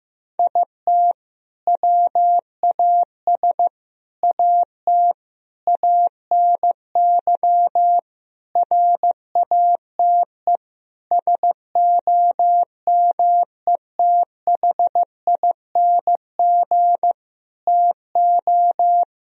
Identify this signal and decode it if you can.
Morse code, heard as a keyed tone, about 15 wpm, 700 Hz. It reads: IT WAS AT ANY RATE SOMETHING TO
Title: Sentences from Top 1000 Words - Review Part 4 - 15wpm